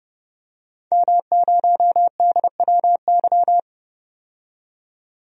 Morse code, heard as a keyed tone, about 30 wpm, 700 Hz. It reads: M0DWY
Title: DXCC-100 Call Signs - 5 Characters - 30wpm